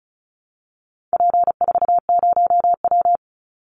Morse code, heard as a keyed tone, about 34 wpm, 700 Hz. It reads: P40W
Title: Call Signs Repeated 3x - 35wpm